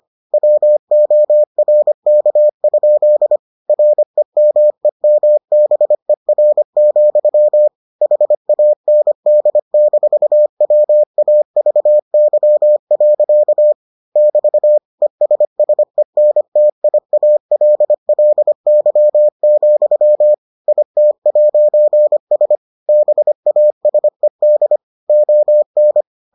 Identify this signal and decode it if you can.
Morse code, heard as a keyed tone, about 25 wpm, 600 Hz. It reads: WORK? REMEMBER, HAND-WAVY. <BT> ESSENTIALLY, IT'S BASED ON